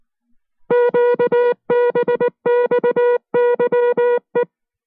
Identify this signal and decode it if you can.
Morse code, heard as a keyed tone, about 19 wpm, 480 Hz. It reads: QBXYE